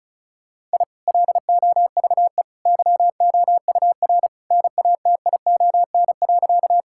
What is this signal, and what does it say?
Morse code, heard as a keyed tone, about 36 wpm, 700 Hz. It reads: I LOVE YOUR NATION.
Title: Sentences from Top 1000 Words - Review Part 6 - 35wpm